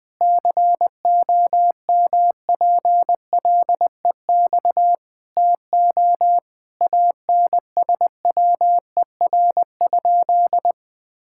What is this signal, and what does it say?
Morse code, heard as a keyed tone, about 20 wpm, 700 Hz. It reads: COMPLEX TO ANSWER?
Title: Sentences from Top 1000 Words - Review Part 6 - 20wpm